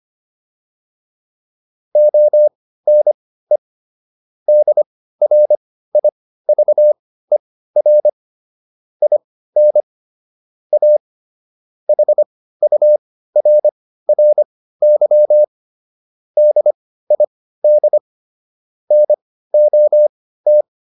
Morse code, heard as a keyed tone, about 25 wpm, spaced as for 15 wpm, 600 Hz. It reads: ONE DRIVER IN A HURRY DID NOT